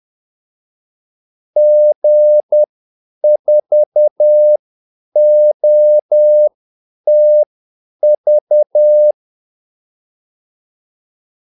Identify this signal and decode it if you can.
Morse code, heard as a keyed tone, about 10 wpm, spaced as for 8 wpm, 600 Hz. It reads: G4OTV
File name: Callsigns-10-8wpm